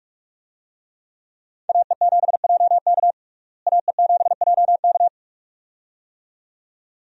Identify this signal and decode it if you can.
Morse code, heard as a keyed tone, about 42 wpm, 700 Hz. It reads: AE7JK AE7JK
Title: USA Call Signs - 5 Characters - Repeated 2x - 45wpm